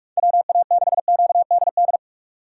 Morse code, heard as a keyed tone, about 47 wpm, 700 Hz. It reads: WA6QDD
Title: USA Call Signs - 6 Characters - 4x Character Spacing - 45wpm